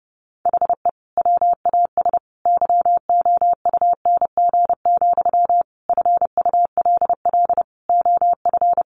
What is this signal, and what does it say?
Morse code, heard as a keyed tone, about 30 wpm, 700 Hz. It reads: HE WAS YOUNG, FULL OF
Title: Sentences from Top 1100 Words 30wpm